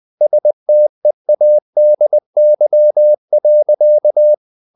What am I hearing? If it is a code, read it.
Morse code, heard as a keyed tone, about 20 wpm, 600 Hz. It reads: STEADY.
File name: the old man and the sea / the old man and the sea part 1 20wpm